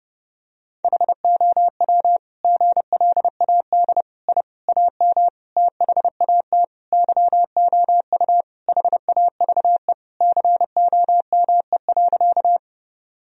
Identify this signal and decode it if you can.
Morse code, heard as a keyed tone, about 32 wpm, 700 Hz. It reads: HOW GLAD I AM THAT YOU HAVE COME.
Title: Sentences from Top 1000 Words - Review Part 2 - 30wpm